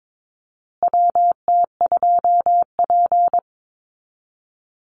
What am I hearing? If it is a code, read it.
Morse code, heard as a keyed tone, about 22 wpm, 700 Hz. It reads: WT2P
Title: Call Signs Repeated 2x - 22wpm